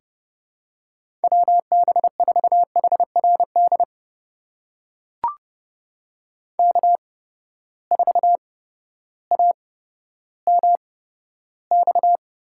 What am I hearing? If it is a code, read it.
Morse code, heard as a keyed tone, about 32 wpm, spaced as for 14 wpm, 700 Hz. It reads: WB4HRD K4AMX
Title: USA Call Signs - 8x Character Spacing - 30wpm